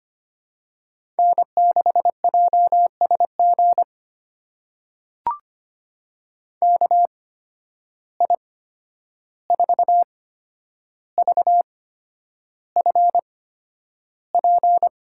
Morse code, heard as a keyed tone, about 25 wpm, spaced as for 10 wpm, 700 Hz. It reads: N6JSG KI4VFP
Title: USA Call Signs - 8x Character Spacing - 25wpm